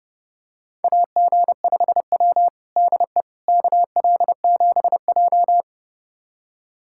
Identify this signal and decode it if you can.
Morse code, heard as a keyed tone, about 30 wpm, 700 Hz. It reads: AG5W DE KL7J
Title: USA Callsign DE Callsign - 4 Characters - 30wpm